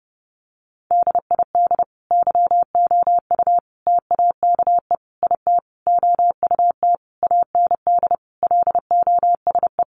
Morse code, heard as a keyed tone, about 30 wpm, 700 Hz. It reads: DID YOU TAKE IT OUT AND LOSE
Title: Sentences from Top 1100 Words 30wpm